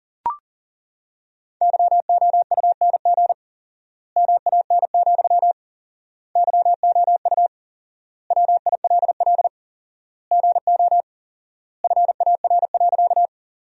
Morse code, heard as a keyed tone, about 40 wpm, 700 Hz. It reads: YOUNG MAN, YOU WILL GO FAR.
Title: 4x Word-Spacing - Sentences from Top 400 Words - 40wpm